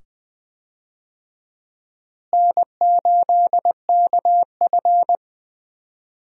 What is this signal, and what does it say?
Morse code, heard as a keyed tone, about 21 wpm, 700 Hz. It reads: N8KF
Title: USA Call Signs - 4 Characters - Repeated 2x - 20wpm